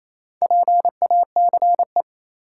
Morse code, heard as a keyed tone, about 28 wpm, 700 Hz. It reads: PACE
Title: Four Letter Words - 28wpm